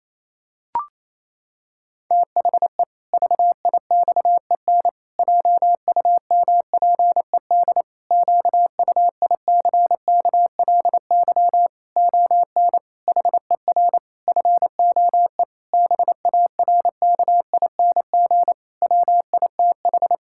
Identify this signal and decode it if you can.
Morse code, heard as a keyed tone, about 28 wpm, 700 Hz. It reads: THE VIXEN JUMPED QUICKLY ON HER FOE BARKING WITH
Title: Pangram Challenge v2 - 28wpm